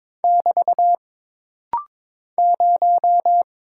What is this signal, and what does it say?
Morse code, heard as a keyed tone, about 22 wpm, 700 Hz. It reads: <BT> 0
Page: This rapid-fire timing is geared to push your ICR (Instant Character Recognition) skills.